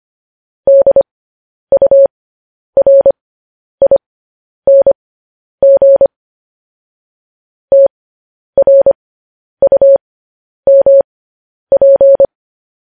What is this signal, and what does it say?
Morse code, heard as a keyed tone, about 25 wpm, spaced as for 10 wpm, 550 Hz. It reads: DURING TRUMP